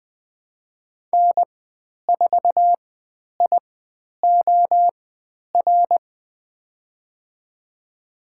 Morse code, heard as a keyed tone, about 20 wpm, spaced as for 10 wpm, 700 Hz. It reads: N4IOR
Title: 102 Lesson 17g - Letter u - Call Signs 20wpm